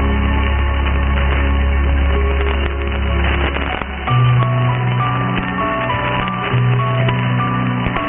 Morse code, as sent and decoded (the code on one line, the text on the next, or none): none